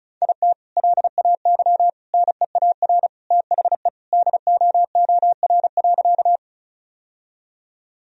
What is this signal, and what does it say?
Morse code, heard as a keyed tone, about 36 wpm, 700 Hz. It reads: IT LAY NEAR THE DOOR.
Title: Sentences from Top 1000 Words - Review Part 2 - 35wpm